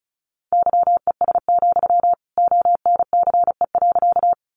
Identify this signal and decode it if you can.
Morse code, heard as a keyed tone, about 34 wpm, 700 Hz. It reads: YES, ONCE.
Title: Sentences from Top 500 Words 35wpm